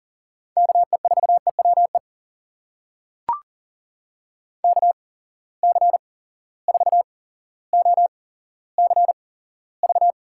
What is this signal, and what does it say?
Morse code, heard as a keyed tone, about 42 wpm, spaced as for 17 wpm, 700 Hz. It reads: KE4EWE KC4OCV
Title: USA Call Signs - 6 Characters - 8x Character Spacing - 40wpm